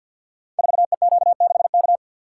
Morse code, heard as a keyed tone, about 50 wpm, 700 Hz. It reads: 4EQ6K
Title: Code Groups - 5 Letter:Numbers - 50wpm